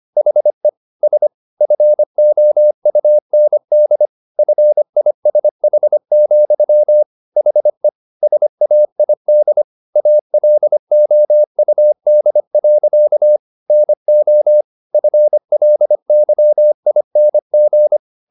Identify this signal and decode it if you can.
Morse code, heard as a keyed tone, about 25 wpm, 600 Hz. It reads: HE S FOUND FISH, HE SAID ALOUD. NO FLYING
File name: the old man and the sea part 1 25wpm